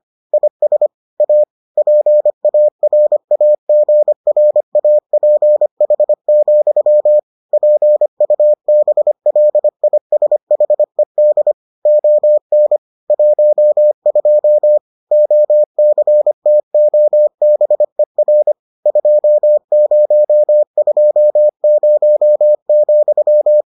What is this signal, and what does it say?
Morse code, heard as a keyed tone, about 25 wpm, 600 Hz. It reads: IS A PARAGRAPH, PUBLISHED ON 12 OCTOBER 2020,